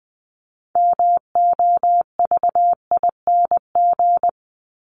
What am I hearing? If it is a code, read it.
Morse code, heard as a keyed tone, about 20 wpm, 700 Hz. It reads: MOVING
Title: ING Suffix - Top 100 Words with Base Words 20wpm